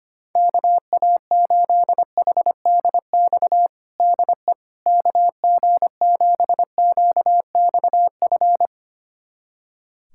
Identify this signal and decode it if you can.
Morse code, heard as a keyed tone, about 25 wpm, 700 Hz. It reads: KA8HDX DE KG7QXF
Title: USA Callsign DE Callsign - 6 Characters - 25wpm